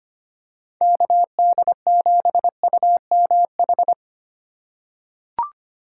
Morse code, heard as a keyed tone, about 25 wpm, 700 Hz. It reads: KD7UMH
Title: USA Call Signs - 6 Characters - 2x Character Spacing - 25wpm